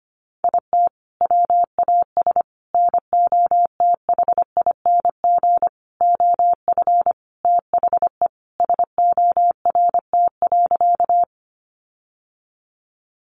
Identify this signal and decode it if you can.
Morse code, heard as a keyed tone, about 25 wpm, 700 Hz. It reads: IT WAS NOTHING OF THE SORT.
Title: Sentences from Top 1300 Words 25wpm